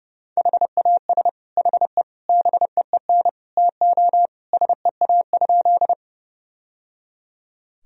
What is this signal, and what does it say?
Morse code, heard as a keyed tone, about 29 wpm, 700 Hz. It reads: HAS HE BEEN TO SEA?
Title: Sentences from Top 1000 Words - Review Part 1 - 30wpm